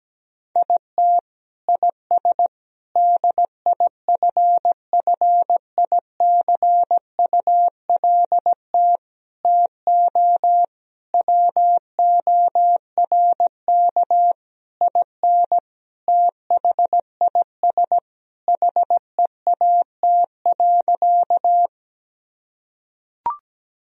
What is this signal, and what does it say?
Morse code, heard as a keyed tone, about 17 wpm, 700 Hz. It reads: IT IS DIFFICULT TO WORK IN THIS HEAT.
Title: Sentences from Top 1000 Words - Review Part 3 - 17wpm